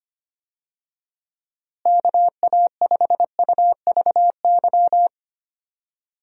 Morse code, heard as a keyed tone, about 25 wpm, 700 Hz. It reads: KA5UVY
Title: USA Call Signs - 6 Characters - Repeated 2x - 25wpm